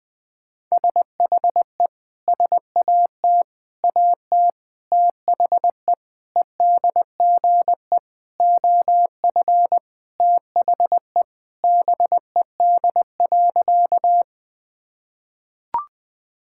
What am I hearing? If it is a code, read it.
Morse code, heard as a keyed tone, about 20 wpm, 700 Hz. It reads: SHE SAT AT THE EDGE OF THE BED.
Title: Sentences with 4-Word Phrases - 20wpm